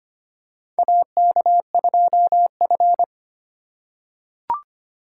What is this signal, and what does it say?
Morse code, heard as a keyed tone, about 25 wpm, 700 Hz. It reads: AK2F
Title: USA Call Signs - 4x Character Spacing - 25wpm